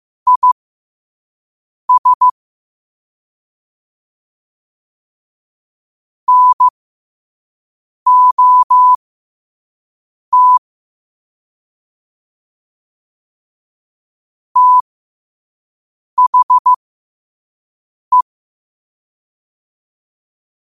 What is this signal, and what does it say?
Morse code, heard as a keyed tone, about 15 wpm, spaced as for 5 wpm, 1000 Hz. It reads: IS NOT THE
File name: qotdfiles / 5wpm